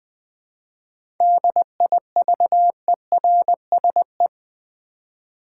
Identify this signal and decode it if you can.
Morse code, heard as a keyed tone, about 20 wpm, 700 Hz. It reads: DIVERSE
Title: 7 Letter Words - First 3 Letters Given - 20wpm